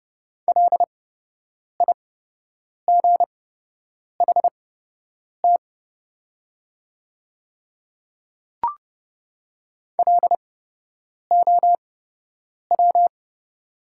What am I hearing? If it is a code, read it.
Morse code, heard as a keyed tone, about 30 wpm, spaced as for 8 wpm, 690 Hz. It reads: LIGHT LOW
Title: 8x Character Spacing - Top 200 Words - 30wpm